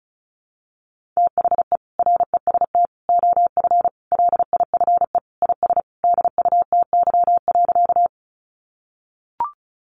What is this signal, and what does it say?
Morse code, heard as a keyed tone, about 34 wpm, 700 Hz. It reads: THE REST OF LIFE IS DUTY.
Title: Sentences from Top 1500 Words 35wpm